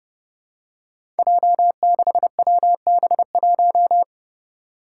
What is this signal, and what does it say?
Morse code, heard as a keyed tone, about 30 wpm, 700 Hz. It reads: J6WB1
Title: Five Challenging Letter-Number Combinations 30wpm